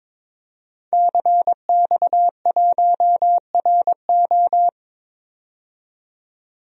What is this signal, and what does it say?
Morse code, heard as a keyed tone, about 22 wpm, 700 Hz. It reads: CX1RO
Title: DXCC-100 Call Signs - 5 Characters - 22wpm